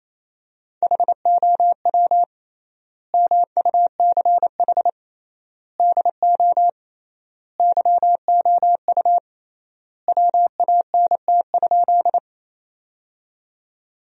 Morse code, heard as a keyed tone, about 28 wpm, 700 Hz. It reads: HOW MUCH DO YOU WANT?
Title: 3x Word-Spacing - Sentences from Top 200 Words - 28wpm